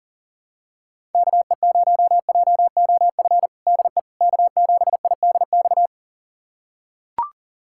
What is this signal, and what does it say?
Morse code, heard as a keyed tone, about 38 wpm, 700 Hz. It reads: KE0JOF DE K7IDX
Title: USA Callsign DE Callsign 40wpm